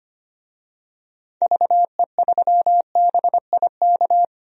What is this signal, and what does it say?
Morse code, heard as a keyed tone, about 25 wpm, 700 Hz. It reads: VE3BIK
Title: DXCC-100 Call Signs - 6 Characters - Repeated 3x - 25wpm